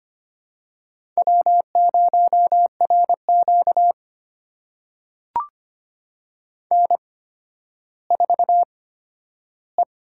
Morse code, heard as a keyed tone, about 25 wpm, spaced as for 12 wpm, 700 Hz. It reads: W0RQ N4E
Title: USA Call Signs - 4 Characters - 8x Character Spacing - 25wpm